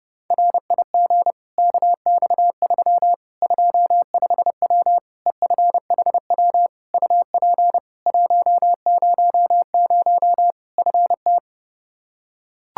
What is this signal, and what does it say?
Morse code, heard as a keyed tone, about 30 wpm, 700 Hz. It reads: RIG KX3 25W EFHW UP 100 FT
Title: Practice QSO - Exchange 2 - Info - 30wpm